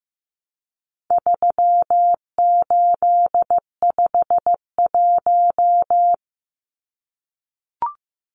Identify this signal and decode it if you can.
Morse code, heard as a keyed tone, about 15 wpm, 700 Hz. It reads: 3851
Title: Numbers - 4-Digit 15wpm